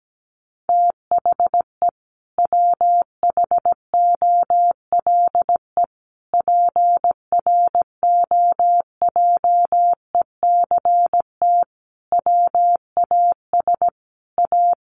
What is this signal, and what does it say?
Morse code, heard as a keyed tone, about 17 wpm, 700 Hz. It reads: THE WHOLE PROJECT WAS A
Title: Sentences from Top 1400 Words 17wpm